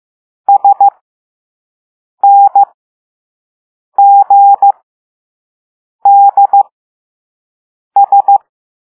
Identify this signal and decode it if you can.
Morse code, heard as a keyed tone, about 15 wpm, spaced as for 5 wpm, 800 Hz. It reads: SNGDS